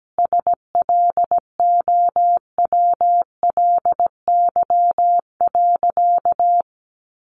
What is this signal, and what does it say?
Morse code, heard as a keyed tone, about 17 wpm, 700 Hz. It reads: SLOWLY.
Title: Sentences from Top 1100 Words 17wpm